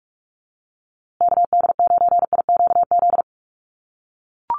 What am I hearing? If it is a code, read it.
Morse code, heard as a keyed tone, about 44 wpm, 700 Hz. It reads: KD9IQZ
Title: USA Call Signs Repeated 2x 45wpm